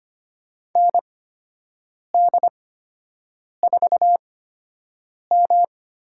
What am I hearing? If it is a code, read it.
Morse code, heard as a keyed tone, about 25 wpm, spaced as for 7 wpm, 700 Hz. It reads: ND4M
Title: USA Call Signs - 4 Characters - 8x Character Spacing - 25wpm